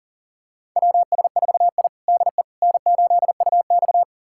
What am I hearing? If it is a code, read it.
Morse code, heard as a keyed tone, about 37 wpm, 700 Hz. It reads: WS4I DE N8UX
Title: USA Callsign DE Callsign - 4 Characters - 40wpm